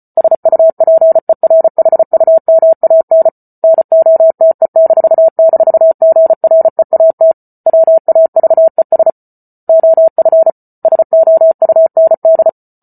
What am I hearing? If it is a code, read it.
Morse code, heard as a keyed tone, about 34 wpm, 650 Hz. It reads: SUPERHUMAN NOTE--GREAT WAVES OF SOUND